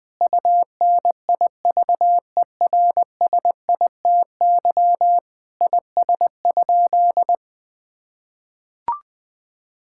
Morse code, heard as a keyed tone, about 20 wpm, 700 Hz. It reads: UNIVERSITY IS?
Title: Sentences from Top 1000 Words - Review Part 5 - 20wpm